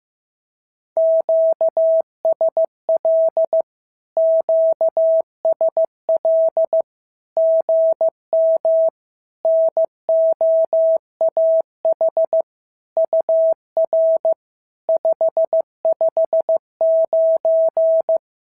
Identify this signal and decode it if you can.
Morse code, heard as a keyed tone, about 15 wpm, 650 Hz. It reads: QSL QSL GM NOAH UR 559
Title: SOTA QSO Practice 15wpm